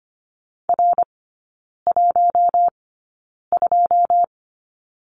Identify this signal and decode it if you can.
Morse code, heard as a keyed tone, about 25 wpm, spaced as for 9 wpm, 700 Hz. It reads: R12
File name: course / 25:10wpm / 093 Lesson 16e - Letter c - Three Characters 25wpm